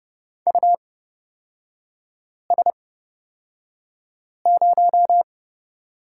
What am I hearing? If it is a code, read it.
Morse code, heard as a keyed tone, about 30 wpm, spaced as for 5 wpm, 700 Hz. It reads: US0
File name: Lesson 37e - Letter x - Three Characters 30wpm